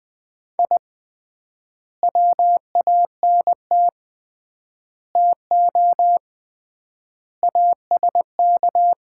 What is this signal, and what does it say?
Morse code, heard as a keyed tone, about 21 wpm, 700 Hz. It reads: I WANT TO ASK
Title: 3x Word-Spacing - Sets of 5 Words - 20wpm